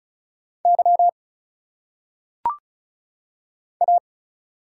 Morse code, heard as a keyed tone, about 36 wpm, 700 Hz. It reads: Y A
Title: Single Letters - Phonetic Pronunciation 35wpm